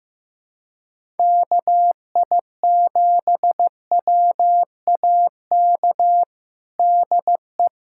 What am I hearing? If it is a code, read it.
Morse code, heard as a keyed tone, about 15 wpm, 700 Hz. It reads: KI7WAK DE
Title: USA Callsign DE Callsign - 6 Characters - 15wpm